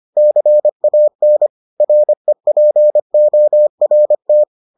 Morse code, heard as a keyed tone, about 25 wpm, 600 Hz. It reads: CAN REPORT